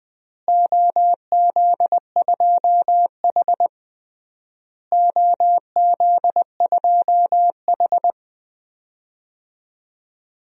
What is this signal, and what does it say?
Morse code, heard as a keyed tone, about 20 wpm, 700 Hz. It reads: OZ2H OZ2H
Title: DXCC-100 Call Signs - 4 Characters - Repeated 3x - 20wpm